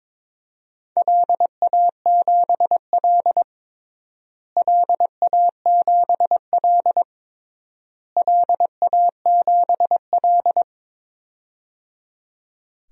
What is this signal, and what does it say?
Morse code, heard as a keyed tone, about 22 wpm, 700 Hz. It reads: LA7L LA7L LA7L
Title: DXCC-100 Call Signs - 4 Characters - Repeated 3x - 22wpm